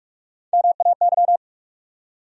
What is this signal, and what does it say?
Morse code, heard as a keyed tone, about 44 wpm, 700 Hz. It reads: MAY
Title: Top 100 Words 45wpm